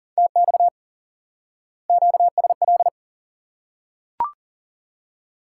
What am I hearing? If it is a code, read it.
Morse code, heard as a keyed tone, about 40 wpm, 700 Hz. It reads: TX QSL
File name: IQE Course 3 - Lesson 89 - YAGI - Two Elements - 40wpm